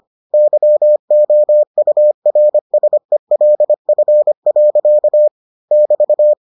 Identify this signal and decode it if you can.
Morse code, heard as a keyed tone, about 25 wpm, 600 Hz. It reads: YOURSELF. <BT>